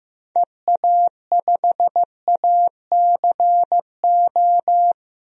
Morse code, heard as a keyed tone, about 15 wpm, 700 Hz. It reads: EA5ACO
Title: DXCC-100 Call Signs - 6 Characters - 15wpm